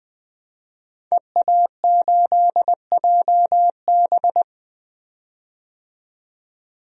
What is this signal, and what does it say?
Morse code, heard as a keyed tone, about 21 wpm, 700 Hz. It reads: EA8JB
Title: DXCC-100 Call Signs - 5 Characters - 20wpm